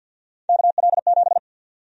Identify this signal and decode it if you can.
Morse code, heard as a keyed tone, about 50 wpm, 700 Hz. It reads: KL7
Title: USA Call Sign Prefixes - 50wpm